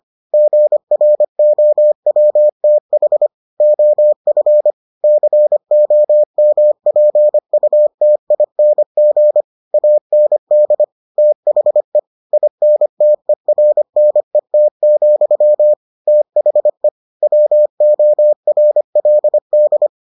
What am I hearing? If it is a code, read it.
Morse code, heard as a keyed tone, about 25 wpm, 600 Hz. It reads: GROWTH OF COMPUTING AND THE INTERNET, THE WORLD